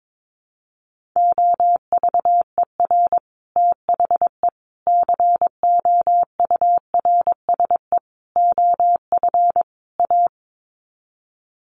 Morse code, heard as a keyed tone, about 22 wpm, 700 Hz. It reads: OVER THE COURSE OF A
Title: Sets of 5 Words Encore 22wpm